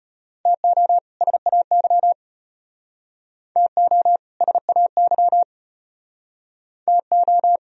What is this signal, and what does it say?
Morse code, heard as a keyed tone, about 32 wpm, 700 Hz. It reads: TO SAY TO SAY TO